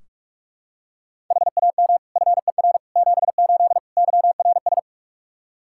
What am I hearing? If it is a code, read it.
Morse code, heard as a keyed tone, about 43 wpm, 700 Hz. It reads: HAM FER 78 YRS
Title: Practice QSO - Exchange 3 - More Info - 45wpm